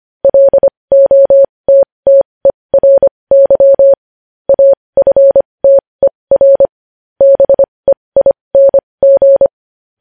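Morse code, heard as a keyed tone, about 25 wpm, spaced as for 20 wpm, 550 Hz. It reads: LOTTERY AFTER BEING
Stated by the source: Morse news headlines practice at 20 wpm.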